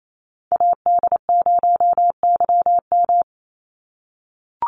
Morse code, heard as a keyed tone, about 28 wpm, 700 Hz. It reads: AD0YM
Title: NA SOTA Chaser Callsigns 28wpm